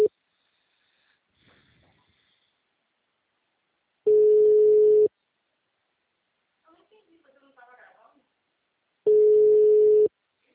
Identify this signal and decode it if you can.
Ringback tone (Europe, 425 Hz), with speech.
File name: out- -2034- - - .67